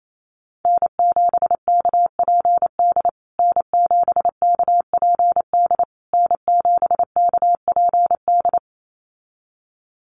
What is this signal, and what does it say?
Morse code, heard as a keyed tone, about 28 wpm, 700 Hz. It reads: N7KPD N7KPD N7KPD
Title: USA Call Signs Repeated 3x 28wpm